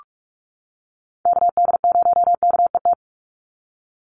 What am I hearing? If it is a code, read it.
Morse code, heard as a keyed tone, about 44 wpm, 700 Hz. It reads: KD0KET
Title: USA Call Signs 45wpm